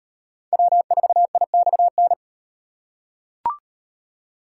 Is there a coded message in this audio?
Yes: Morse code from 0.5 to 2.1 s, heard as a keyed tone, about 37 wpm, 700 Hz. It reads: W4IXN